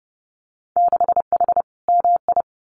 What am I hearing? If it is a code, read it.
Morse code, heard as a keyed tone, about 30 wpm, 700 Hz. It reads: 6H MI
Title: Winter Field Day Exchanges Encore 30wpm